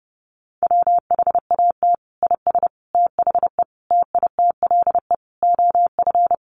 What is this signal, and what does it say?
Morse code, heard as a keyed tone, about 30 wpm, 700 Hz. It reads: WHAT IS THE TITLE OF
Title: Sentences from Top 1200 Words 30wpm